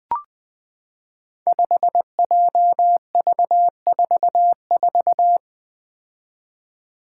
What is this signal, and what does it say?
Morse code, heard as a keyed tone, about 20 wpm, 700 Hz. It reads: 5JV44
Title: Five Challenging Letter-Number Combinations 20wpm